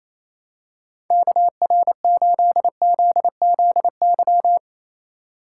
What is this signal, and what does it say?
Morse code, heard as a keyed tone, about 27 wpm, 700 Hz. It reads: KR8ZZY
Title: USA Call Signs - 6 Characters - 4x Character Spacing - 28wpm